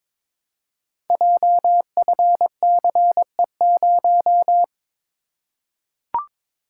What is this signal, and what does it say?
Morse code, heard as a keyed tone, about 22 wpm, 700 Hz. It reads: JFCE0
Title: Code Groups - 5 Letter:Numbers - 22wpm